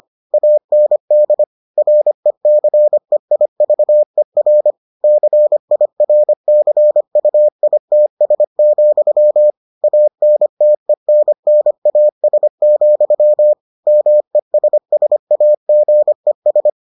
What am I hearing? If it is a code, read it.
Morse code, heard as a keyed tone, about 25 wpm, 600 Hz. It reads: AND RECEIVER CIRCUITS, ANTENNAS, MESSAGES